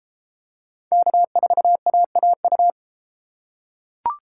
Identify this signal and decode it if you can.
Morse code, heard as a keyed tone, about 30 wpm, 700 Hz. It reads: K4AAU